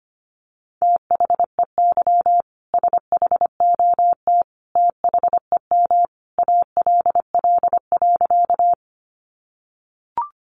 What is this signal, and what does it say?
Morse code, heard as a keyed tone, about 25 wpm, 700 Hz. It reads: THEY SHOT THEM ALL.
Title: Sentences from Top 1100 Words 25wpm